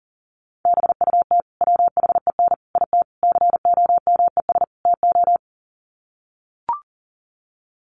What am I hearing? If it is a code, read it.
Morse code, heard as a keyed tone, about 40 wpm, 700 Hz. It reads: BUT WHEN IT COMES TO
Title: Sets of 5 Words 40wpm